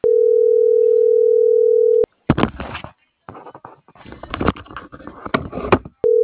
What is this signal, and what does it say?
Ringback tone (North America), 440 + 480 Hz.